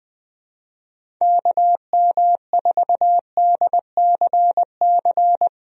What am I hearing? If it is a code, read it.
Morse code, heard as a keyed tone, about 20 wpm, 700 Hz. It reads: KM4DCC
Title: USA Call Signs - 6 Characters - Repeated 3x - 20wpm